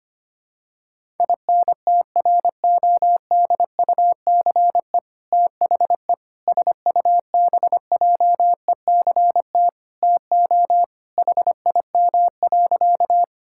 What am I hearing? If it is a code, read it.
Morse code, heard as a keyed tone, about 25 wpm, 700 Hz. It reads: INTRODUCE THE SUBJECT TO HIM.
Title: Sentences from Top 1600 Words 25wpm